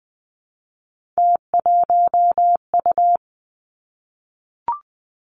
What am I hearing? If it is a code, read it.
Morse code, heard as a keyed tone, about 20 wpm, 700 Hz. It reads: T1U
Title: Three Challenging Letter-Number Combinations 20wpm